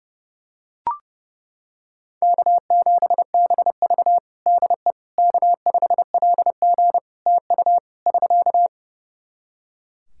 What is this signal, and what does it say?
Morse code, heard as a keyed tone, about 29 wpm, 700 Hz. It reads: K7BV DE K5LG TU <SK>
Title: K1USN SST QSO Practice 30wpm